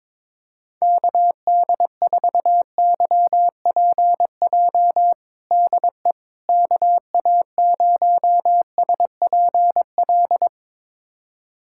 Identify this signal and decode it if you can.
Morse code, heard as a keyed tone, about 22 wpm, 700 Hz. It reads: KD4YPJ DE KA0SPL
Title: USA Callsign DE Callsign - 6 Characters - 22wpm